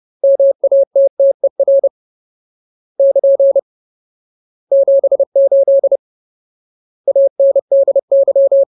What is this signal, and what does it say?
Morse code, heard as a keyed tone, about 30 wpm, 550 Hz. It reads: MATTER <KN> 78 ANDY
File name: Ultimate Word List / 30WPM Ultimate